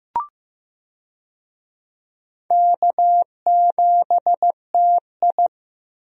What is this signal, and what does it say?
Morse code, heard as a keyed tone, about 15 wpm, 700 Hz. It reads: K7TI
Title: USA Call Signs - 4 Characters - Repeated 3x - 15wpm